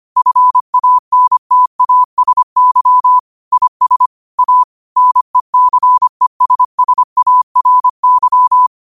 Morse code, heard as a keyed tone, about 25 wpm, 1000 Hz. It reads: FANTASY IS A NECESSARY
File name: qotdfiles / 25wpm